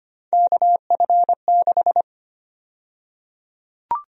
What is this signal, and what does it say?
Morse code, heard as a keyed tone, about 25 wpm, 700 Hz. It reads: KF6
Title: USA Call Sign Prefixes - Repeated 2x - 25wpm